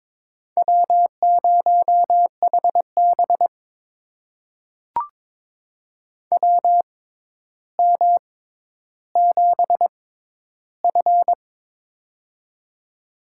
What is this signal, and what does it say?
Morse code, heard as a keyed tone, about 22 wpm, spaced as for 11 wpm, 700 Hz. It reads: W0HB WM7F
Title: USA Call Signs - 4 Characters - 6x Character Spacing - 22wpm